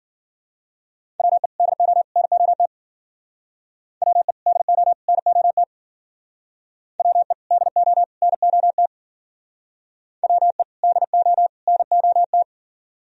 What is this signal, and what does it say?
Morse code, heard as a keyed tone, about 50 wpm, 700 Hz. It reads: WE DO NOT WE DO NOT WE DO NOT WE DO NOT
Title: Speed-Racing - Sets of 3 Words 40wpm